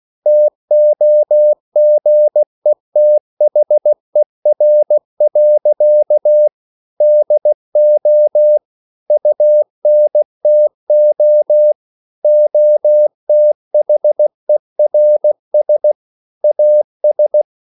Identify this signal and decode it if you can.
Morse code, heard as a keyed tone, about 16 wpm, 600 Hz. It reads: TOGETHER. DO UNTO OTHERS AS
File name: CW/Practice